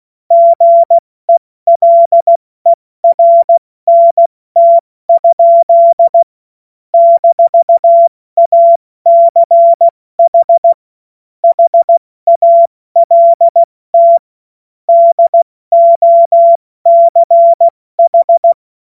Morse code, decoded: GELERNT? - ACH HALT DOCH

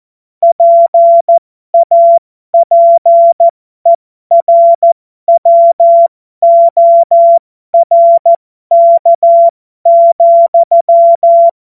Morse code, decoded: PAPERWORK,

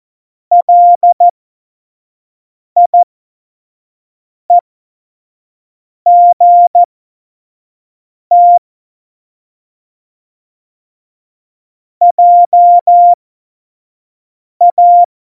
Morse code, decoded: LIEGT JA

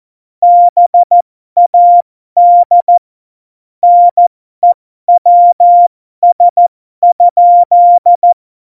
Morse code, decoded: BAD NEWS?

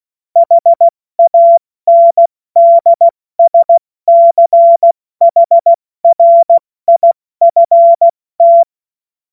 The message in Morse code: HANDSCHRIFT

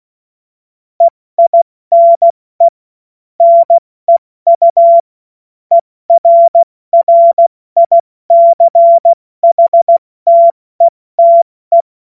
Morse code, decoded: EINE NEU ERRICHTETE

